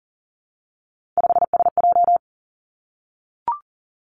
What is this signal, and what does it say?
Morse code, heard as a keyed tone, about 38 wpm, 700 Hz. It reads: 5SJ
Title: Three Challenging Letter-Number Combinations 40wpm